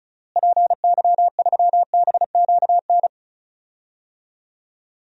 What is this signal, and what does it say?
Morse code, heard as a keyed tone, about 34 wpm, 700 Hz. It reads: PY3BQN
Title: DXCC-100 Call Signs - 6 Characters - Repeated 2x - 35wpm